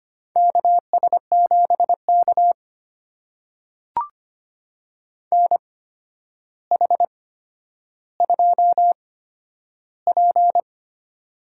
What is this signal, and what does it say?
Morse code, heard as a keyed tone, about 25 wpm, spaced as for 11 wpm, 700 Hz. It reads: KS7K NH2P
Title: USA Call Signs - 8x Character Spacing - 25wpm